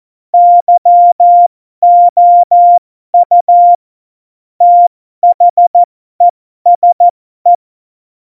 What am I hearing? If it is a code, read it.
Morse code, heard as a keyed tone, about 14 wpm, spaced as for 12 wpm, 700 Hz. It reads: YOU THESE
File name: Morsekurs / Witze